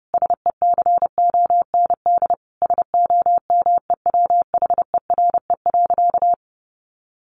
Morse code, heard as a keyed tone, about 30 wpm, 700 Hz. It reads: SECOND SOMEWHERE.
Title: Sentences from Top 1400 Words 30wpm